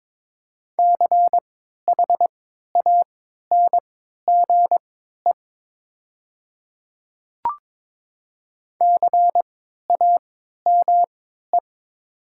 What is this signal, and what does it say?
Morse code, heard as a keyed tone, about 22 wpm, spaced as for 12 wpm, 700 Hz. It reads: CHANGE CAME
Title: 3x Character Spacing - Top 200 Words - 22wpm